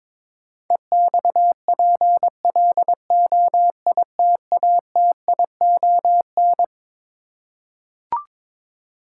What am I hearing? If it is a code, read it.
Morse code, heard as a keyed tone, about 22 wpm, 700 Hz. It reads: EXPLOITATION
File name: Twelve Letter Words - 22wpm